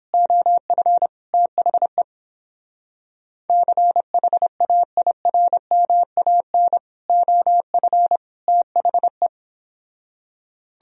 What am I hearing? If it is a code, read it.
Morse code, heard as a keyed tone, about 27 wpm, 700 Hz. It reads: OF THE CHAIRMAN OF THE
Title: Speed-Racing - Sets of 3 Words 22wpm